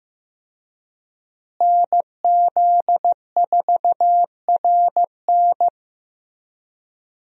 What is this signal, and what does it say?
Morse code, heard as a keyed tone, about 15 wpm, 700 Hz. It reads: NZ4RN
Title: USA Call Signs - 5 Characters - Repeated 2x - 15wpm